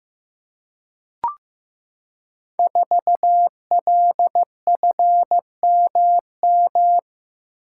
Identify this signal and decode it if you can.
Morse code, heard as a keyed tone, about 15 wpm, 700 Hz. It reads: 4LFMM
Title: Code Groups - 5 Letter:Numbers - 15wpm